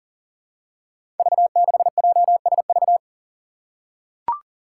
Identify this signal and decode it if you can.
Morse code, heard as a keyed tone, about 40 wpm, 700 Hz. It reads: V6JSV
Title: Five Challenging Letter-Number Combinations 40wpm